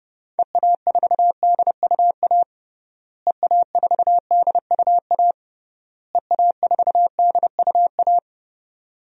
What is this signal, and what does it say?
Morse code, heard as a keyed tone, about 30 wpm, 700 Hz. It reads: EA4DUA EA4DUA EA4DUA
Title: DXCC-100 Call Signs - 6 Characters - Repeated 3x - 30wpm